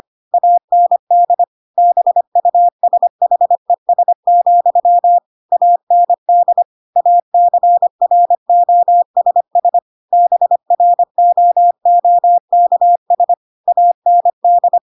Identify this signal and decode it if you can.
Morse code, heard as a keyed tone, about 25 wpm, 700 Hz. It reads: AND BUSHES, AND ACROSS BROOKS AND
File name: Pinocchio / Pinocchio ch4 25 25 ik1hge